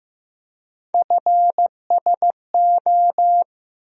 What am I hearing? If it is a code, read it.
Morse code, heard as a keyed tone, about 15 wpm, 700 Hz. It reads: FSO
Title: USA Call Sign Postfixes - 3 Letters 15wpm